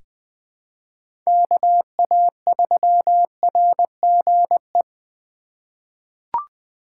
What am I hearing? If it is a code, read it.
Morse code, heard as a keyed tone, about 20 wpm, 700 Hz. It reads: KA3RGE
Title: USA Call Signs - 6 Characters - 2x Character Spacing - 20wpm